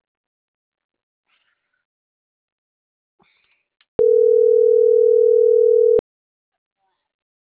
Ringback tone (North America), 440 + 480 Hz.